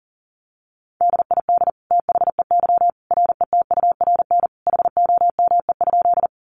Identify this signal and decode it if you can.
Morse code, heard as a keyed tone, about 40 wpm, 700 Hz. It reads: DID THEY RETURN HOME?
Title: Sentences from Top 800 Words 40wpm